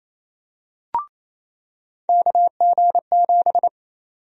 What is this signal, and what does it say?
Morse code, heard as a keyed tone, about 27 wpm, 700 Hz. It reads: KG7
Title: USA Call Sign Prefixes - 28wpm